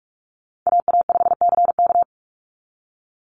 Morse code, heard as a keyed tone, about 46 wpm, 700 Hz. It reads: AA5CK
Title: NA SOTA Chaser Callsigns 45wpm